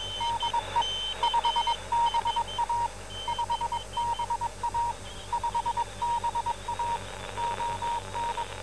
Morse code, heard as a keyed tone, about 21 wpm, 940 Hz, with noise. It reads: DE 5BA 5BA 5BA ON